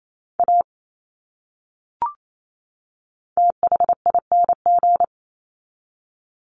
Morse code, heard as a keyed tone, about 28 wpm, 700 Hz. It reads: A THING
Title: Top 200 Words 28wpm